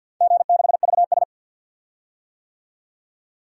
Morse code, heard as a keyed tone, about 51 wpm, 700 Hz. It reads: G6VS
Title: DXCC-100 Call Signs - 4 Characters - 50wpm